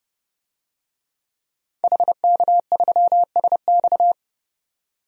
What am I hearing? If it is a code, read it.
Morse code, heard as a keyed tone, about 30 wpm, 700 Hz. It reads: HK3SX
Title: DXCC-100 Call Signs - 5 Characters - 30wpm